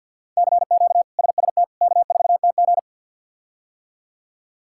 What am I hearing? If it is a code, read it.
Morse code, heard as a keyed tone, about 51 wpm, 700 Hz. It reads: CQ SST K4TG
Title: K1USN SST QSO Practice - 50wpm